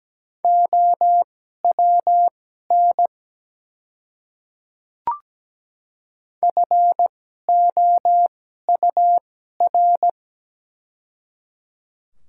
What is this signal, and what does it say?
Morse code, heard as a keyed tone, about 17 wpm, spaced as for 12 wpm, 700 Hz. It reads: OWN FOUR
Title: 2x Character Spacing - Top 300 Words - 17wpm